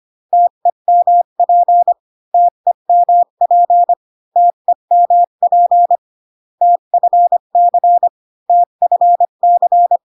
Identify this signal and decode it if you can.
Morse code, heard as a keyed tone, about 25 wpm, 700 Hz. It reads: TEMP TEMP TEMP TFC TFC